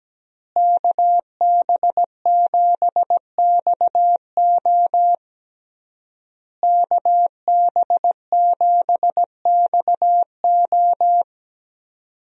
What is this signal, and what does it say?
Morse code, heard as a keyed tone, about 17 wpm, 700 Hz. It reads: KB7XO KB7XO
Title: USA Call Signs - 5 Characters - Repeated 3x - 17wpm